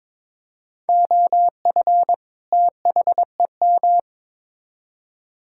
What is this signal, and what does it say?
Morse code, heard as a keyed tone, about 22 wpm, 700 Hz. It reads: OF THEM
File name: practice / 22wpm / Build A Phrase - 2-Word Phrases - 22wpm